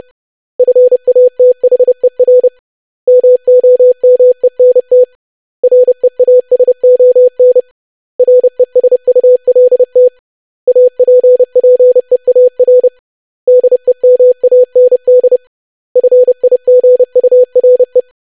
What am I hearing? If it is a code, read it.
Morse code, heard as a keyed tone, about 30 wpm, 500 Hz. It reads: FATHER MOMENT REASON RESULT APPEAR DEMAND FIGURE
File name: lesson / web / basic500-25wpm